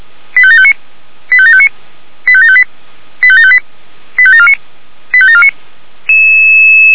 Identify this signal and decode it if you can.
Five-tone selective call (ZVEI), heard as quick runs of five tones, tones of about 70 ms. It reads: 85759 85759 85758 85758 85749 85749